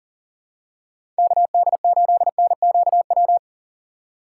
Morse code, heard as a keyed tone, about 37 wpm, 700 Hz. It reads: KD8NQW
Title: USA Call Signs - 2x Character Spacing - 40wpm